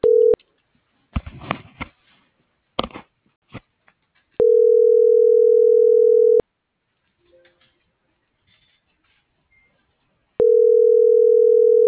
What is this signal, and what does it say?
Ringback tone (North America), 440 + 480 Hz.